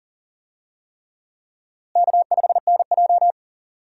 Morse code, heard as a keyed tone, about 40 wpm, 700 Hz. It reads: K5NJ
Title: USA Call Signs - 4 Characters - Repeated 3x - 40wpm